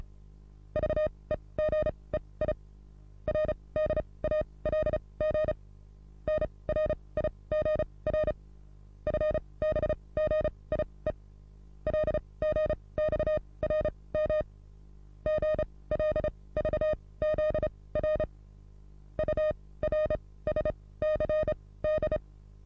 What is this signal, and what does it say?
Morse code, heard as a keyed tone, about 34 wpm, spaced as for 22 wpm, 620 Hz. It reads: VEGEI RDALG NRIGR FBGIE LGXRM GLVZR URSCD
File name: repos / Morsekurs / DJ4UF Morsen27